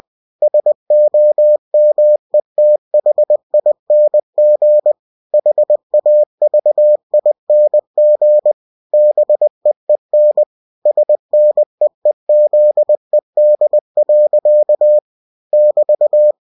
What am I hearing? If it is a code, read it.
Morse code, heard as a keyed tone, about 20 wpm, 600 Hz. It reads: SOMETHING HAVING BEEN SNEEZED. <BT>
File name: cw-books / winnie-the-pooh / winnie-20wpm-page-0030